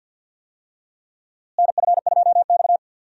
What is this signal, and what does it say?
Morse code, heard as a keyed tone, about 52 wpm, 700 Hz. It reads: NF2X
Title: USA Call Signs - 4 Characters - Repeated 2x - 50wpm